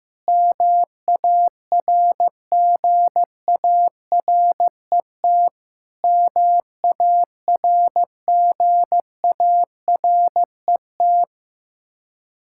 Morse code, heard as a keyed tone, about 15 wpm, 700 Hz. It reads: MARGARET MARGARET
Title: Practice QSO - Exchange 1 - Beginning - 15wpm